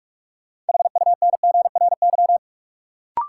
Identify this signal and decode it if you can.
Morse code, heard as a keyed tone, about 46 wpm, 700 Hz. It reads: HUNGRY
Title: Sentences from Top 1400 Words - New Word Drill Alt - 45wpm